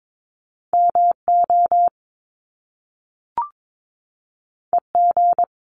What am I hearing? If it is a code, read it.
Morse code, heard as a keyed tone, about 22 wpm, 700 Hz. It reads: MO EG
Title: Top Two Letter Combinations in Words 22wpm